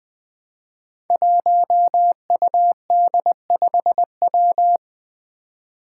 Morse code, heard as a keyed tone, about 20 wpm, 700 Hz. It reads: 1UD5W
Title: Five Challenging Letter-Number Combinations 20wpm